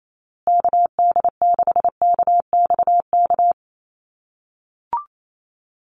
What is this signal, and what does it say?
Morse code, heard as a keyed tone, about 28 wpm, 700 Hz. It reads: KD6KXK
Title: USA Call Signs Repeated 2x 28wpm